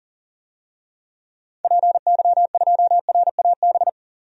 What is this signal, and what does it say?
Morse code, heard as a keyed tone, about 37 wpm, 700 Hz. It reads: PY2RAB
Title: DXCC-100 Call Signs - 6 Characters - 40wpm